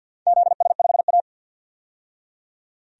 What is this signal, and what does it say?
Morse code, heard as a keyed tone, about 52 wpm, 700 Hz. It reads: ZS5A
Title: DXCC-100 Call Signs - 4 Characters - 50wpm